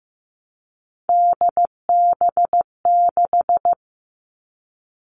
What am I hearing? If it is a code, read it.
Morse code, heard as a keyed tone, about 15 wpm, 700 Hz. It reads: DB6
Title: All Cut Numbers - 3-Digit 15wpm